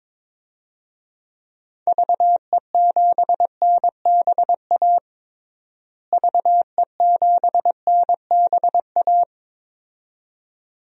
Morse code, heard as a keyed tone, about 22 wpm, 700 Hz. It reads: VE7NBA VE7NBA
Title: DXCC-100 Call Signs - 6 Characters - Repeated 2x - 22wpm